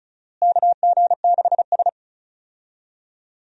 Morse code, heard as a keyed tone, about 36 wpm, 700 Hz. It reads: KG6S
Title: USA Call Signs - 4 Characters - Repeated 3x - 35wpm